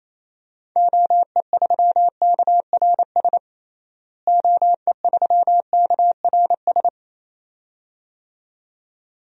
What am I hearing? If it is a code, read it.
Morse code, heard as a keyed tone, about 27 wpm, 700 Hz. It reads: OE3KRS OE3KRS
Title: DXCC-100 Call Signs - 6 Characters - Repeated 2x - 28wpm